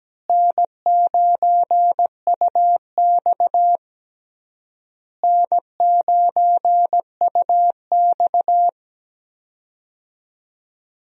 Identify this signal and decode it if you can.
Morse code, heard as a keyed tone, about 17 wpm, 700 Hz. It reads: N9UX N9UX
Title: USA Call Signs - 4 Characters - Repeated 3x - 17wpm